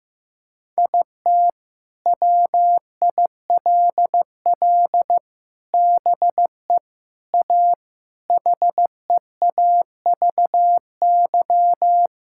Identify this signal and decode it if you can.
Morse code, heard as a keyed tone, about 15 wpm, 700 Hz. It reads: IT WILL BE A HEAVY
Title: Sentences from Top 1600 Words 15wpm